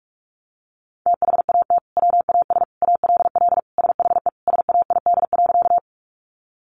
Morse code, heard as a keyed tone, about 44 wpm, 700 Hz. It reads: THAT WAS ALL SHE SAID.
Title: Sentences from Top 100 Words 45wpm